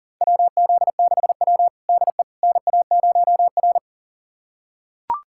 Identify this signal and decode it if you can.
Morse code, heard as a keyed tone, about 40 wpm, 700 Hz. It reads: WZ6W DE NA0R
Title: USA Callsign DE Callsign - 4 Characters - 40wpm